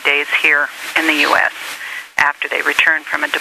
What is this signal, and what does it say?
Emergency Alert System (SAME) tones, heard as harsh buzzing bursts, with speech. It is heard from 0.5 to 3.2 s.